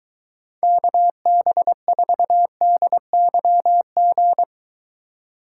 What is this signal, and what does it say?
Morse code, heard as a keyed tone, about 23 wpm, 700 Hz. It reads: KB4DYG